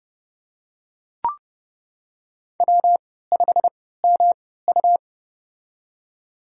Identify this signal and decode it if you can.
Morse code, heard as a keyed tone, about 30 wpm, spaced as for 17 wpm, 700 Hz. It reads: W5MU
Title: USA Call Signs - 3x Character Spacing - 30wpm